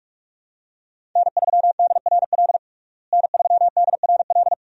Morse code, heard as a keyed tone, about 46 wpm, 700 Hz. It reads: N3DRL N3DRL
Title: USA Call Signs - 5 Characters - Repeated 2x - 45wpm